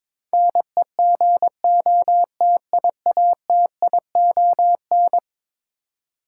Morse code, heard as a keyed tone, about 22 wpm, 700 Hz. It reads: NEGOTIATION
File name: practice / 22wpm / Eleven Letter Words - 22wpm